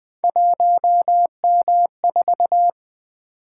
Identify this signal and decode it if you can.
Morse code, heard as a keyed tone, about 20 wpm, 700 Hz. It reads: 1M4
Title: Three Letter-Number Combinations - 20wpm